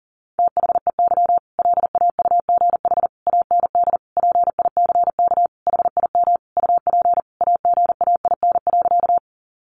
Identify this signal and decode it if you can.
Morse code, heard as a keyed tone, about 40 wpm, 700 Hz. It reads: THEY LAUGH AND PICK HIM UP AGAIN.
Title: Sentences from Top 1000 Words 40wpm